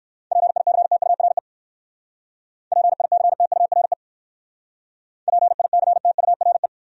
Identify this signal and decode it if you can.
Morse code, heard as a keyed tone, about 66 wpm, 700 Hz. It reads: PICTURE PICTURE PICTURE